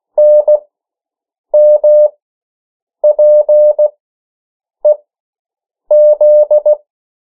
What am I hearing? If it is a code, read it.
Morse code, heard as a keyed tone, about 16 wpm, spaced as for 7 wpm, 600 Hz. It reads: NMPEZ
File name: morse mp3 / 14 urok 13